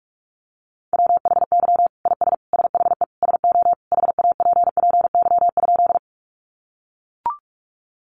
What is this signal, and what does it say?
Morse code, heard as a keyed tone, about 44 wpm, 700 Hz. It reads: WHY IS SHE SO HAPPY?
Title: Sentences from Top 1000 Words 45wpm